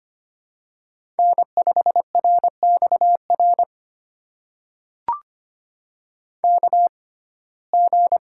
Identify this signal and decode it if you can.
Morse code, heard as a keyed tone, about 25 wpm, spaced as for 18 wpm, 700 Hz. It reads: N5RXR KG
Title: USA Call Signs - 6x Character Spacing - 25wpm